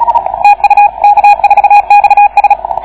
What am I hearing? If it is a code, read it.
Morse code, heard as a keyed tone, about 37 wpm, 810 Hz, with radio noise. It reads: TU K4XS